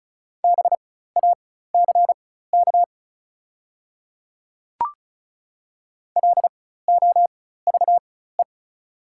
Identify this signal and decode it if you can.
Morse code, heard as a keyed tone, about 36 wpm, spaced as for 16 wpm, 690 Hz. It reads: BACK LOVE
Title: 4x Character Spacing - Top 500 Words - 35wpm